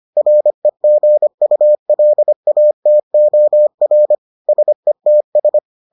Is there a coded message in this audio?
Yes: Morse code from 0.2 to 5.6 s, heard as a keyed tone, about 25 wpm, 600 Hz. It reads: REGULATOR SETS